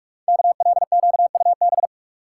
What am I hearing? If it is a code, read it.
Morse code, heard as a keyed tone, about 44 wpm, 700 Hz. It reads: KLQUB
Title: Code Groups - 5 Letter:Numbers - 45wpm